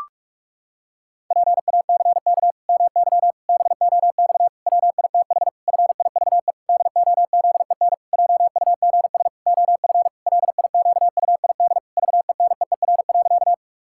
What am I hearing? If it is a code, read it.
Morse code, heard as a keyed tone, about 43 wpm, 700 Hz. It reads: PACK MY BOX WITH FIVE DOZEN JUGS OF LIQUID VENEER.
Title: Pangram Challenge v2 - 45wpm